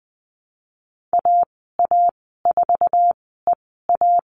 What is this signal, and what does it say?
Morse code, heard as a keyed tone, about 20 wpm, spaced as for 14 wpm, 700 Hz. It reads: AA4EA
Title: 062 Lesson 11g - Letter h - Call Signs 20wpm